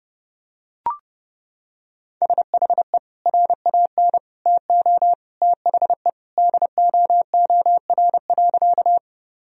Morse code, heard as a keyed tone, about 31 wpm, 700 Hz. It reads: SHE RAN TO THE DOOR.
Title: Sentences from Top 1000 Words - Review Part 3 - 30wpm